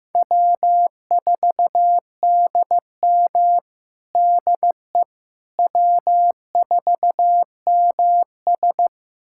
Morse code, heard as a keyed tone, about 15 wpm, 700 Hz. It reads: W4DM DE W4MS
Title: USA Callsign DE Callsign - 4 Characters - 15wpm